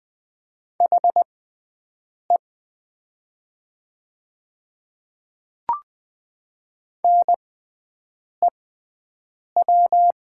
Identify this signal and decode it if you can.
Morse code, heard as a keyed tone, about 20 wpm, spaced as for 7 wpm, 700 Hz. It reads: HE NEW
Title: 6x Character Spacing - Top 300 Words - 20wpm